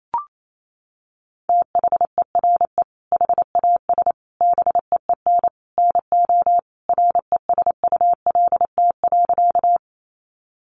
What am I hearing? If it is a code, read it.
Morse code, heard as a keyed tone, about 28 wpm, 700 Hz. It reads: THERE HAS BEEN NO RESULT.
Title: Sentences from Top 700 Words 28wpm